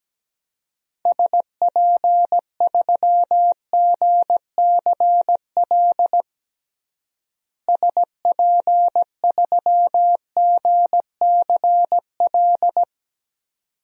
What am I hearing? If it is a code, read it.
Morse code, heard as a keyed tone, about 17 wpm, 700 Hz. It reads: SP3GCL SP3GCL
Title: DXCC-100 Call Signs - 6 Characters - Repeated 2x - 17wpm